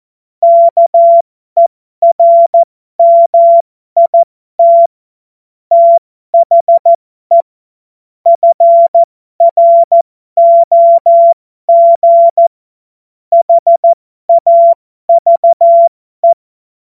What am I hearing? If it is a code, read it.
Morse code, heard as a keyed tone, about 14 wpm, spaced as for 12 wpm, 680 Hz. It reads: KERMIT THE FROG HAVE
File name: Morsekurs / Witze